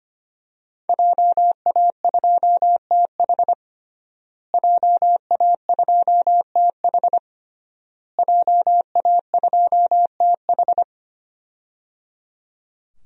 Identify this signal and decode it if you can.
Morse code, heard as a keyed tone, about 25 wpm, 700 Hz. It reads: JA2TH JA2TH JA2TH
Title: DXCC-100 Call Signs - 5 Characters - Repeated 3x - 25wpm